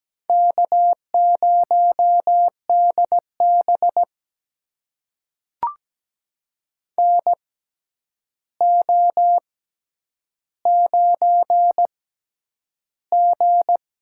Morse code, heard as a keyed tone, about 17 wpm, spaced as for 9 wpm, 700 Hz. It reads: K0DB NO9G
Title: USA Call Signs - 4 Characters - 6x Character Spacing - 17wpm